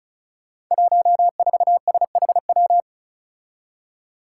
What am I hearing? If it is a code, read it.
Morse code, heard as a keyed tone, about 36 wpm, 700 Hz. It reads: 14SHW
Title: Five Challenging Letter-Number Combinations 35wpm